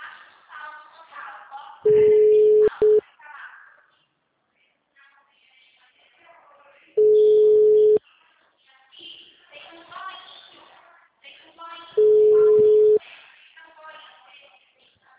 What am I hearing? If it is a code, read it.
Ringback tone (Europe, 425 Hz), with speech.
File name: out- -2004- - - .0